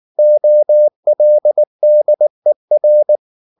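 Morse code, heard as a keyed tone, about 19 wpm, 600 Hz. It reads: OLDER
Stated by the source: Morse code at 19 wpm from CW med Gnister, reading Ken Follett's novel Never in English.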